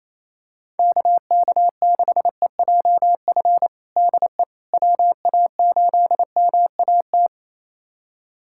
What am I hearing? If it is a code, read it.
Morse code, heard as a keyed tone, about 28 wpm, 700 Hz. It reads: KK6EJF DE WA8MAT
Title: USA Callsign DE Callsign 28wpm